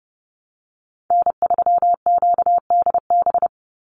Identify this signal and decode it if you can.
Morse code, heard as a keyed tone, about 30 wpm, 700 Hz. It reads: N3QDB
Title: USA Call Signs 30wpm